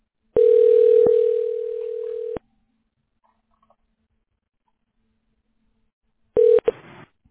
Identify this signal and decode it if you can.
Ringback tone (North America), 440 + 480 Hz.